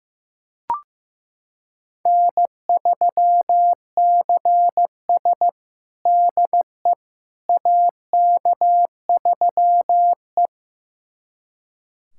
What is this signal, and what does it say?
Morse code, heard as a keyed tone, about 15 wpm, 700 Hz. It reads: N3CS DE AK3E
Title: USA Callsign DE Callsign - 4 Characters - 15wpm